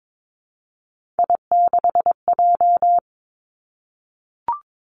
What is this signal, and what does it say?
Morse code, heard as a keyed tone, about 22 wpm, 700 Hz. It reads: I6J
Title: Three Challenging Letter-Number Combinations 22wpm